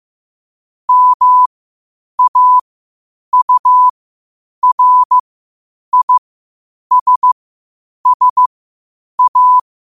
Morse code, heard as a keyed tone, about 15 wpm, spaced as for 8 wpm, 1000 Hz. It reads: MAURISSA
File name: qotdfiles / 8wpm